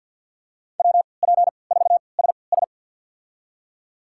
Morse code, heard as a keyed tone, about 46 wpm, spaced as for 27 wpm, 700 Hz. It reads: WP4SS
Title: USA Call Signs - 5 Characters - 3x Character Spacing - 50wpm